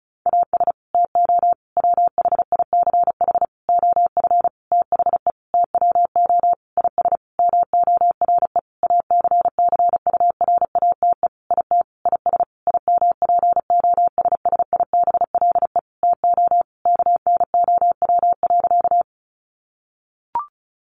Morse code, heard as a keyed tone, about 34 wpm, 700 Hz. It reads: AS TO WHICH OF THE TWO IS MORE ACCURATE IT IS IMPOSSIBLE TO KNOW.
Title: Sentences from Top 1200 Words 35wpm